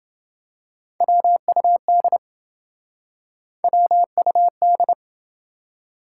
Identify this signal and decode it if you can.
Morse code, heard as a keyed tone, about 27 wpm, 700 Hz. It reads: WUD WUD